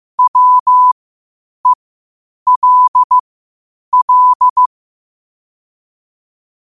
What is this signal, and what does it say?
Morse code, heard as a keyed tone, about 15 wpm, spaced as for 8 wpm, 1000 Hz. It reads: WELL
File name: qotdfiles / 8wpm